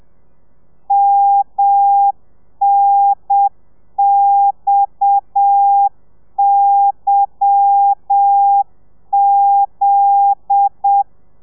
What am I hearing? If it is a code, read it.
Morse code, heard as a keyed tone, about 7 wpm, 800 Hz. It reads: MNXYZ
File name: telegrafia / i2tfj / Lez10 30